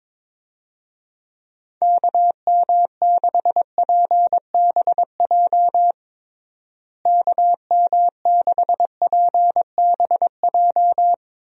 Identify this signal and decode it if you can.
Morse code, heard as a keyed tone, about 22 wpm, 700 Hz. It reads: KM6PBJ KM6PBJ
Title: USA Call Signs - 6 Characters - Repeated 2x - 22wpm